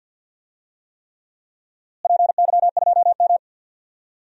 Morse code, heard as a keyed tone, about 52 wpm, 700 Hz. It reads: PY2M
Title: DXCC-100 Call Signs - 4 Characters - Repeated 3x - 50wpm